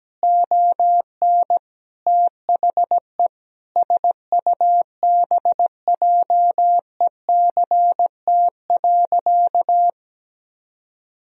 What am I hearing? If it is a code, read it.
Morse code, heard as a keyed tone, about 17 wpm, 700 Hz. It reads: ON THE SUBJECT.
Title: Sentences from Top 1000 Words - Review Part 3 - 17wpm